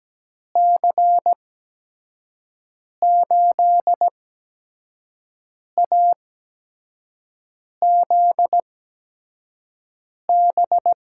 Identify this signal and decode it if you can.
Morse code, heard as a keyed tone, about 17 wpm, spaced as for 5 wpm, 700 Hz. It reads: C8AZB
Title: USA Call Signs - 6 Characters - 8x Character Spacing - 17wpm